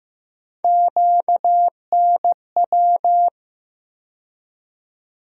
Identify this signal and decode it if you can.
Morse code, heard as a keyed tone, about 15 wpm, 700 Hz. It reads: QNW
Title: USA Call Sign Postfixes - 3 Letters 15wpm